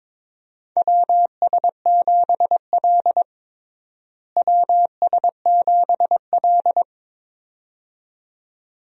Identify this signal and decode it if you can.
Morse code, heard as a keyed tone, about 22 wpm, 700 Hz. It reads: WS7L WS7L
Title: USA Call Signs - 4 Characters - Repeated 2x - 22wpm